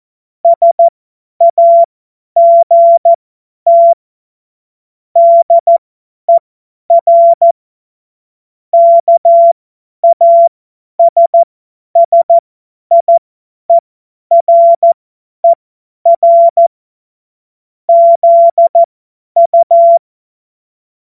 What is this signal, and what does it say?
Morse code, heard as a keyed tone, about 14 wpm, spaced as for 10 wpm, 680 Hz. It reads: SAGT DER KASSIERER ZU